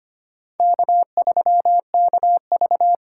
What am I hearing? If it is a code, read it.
Morse code, heard as a keyed tone, about 25 wpm, 700 Hz. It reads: K3KV
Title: USA Call Signs - 4 Characters - 2x Character Spacing - 25wpm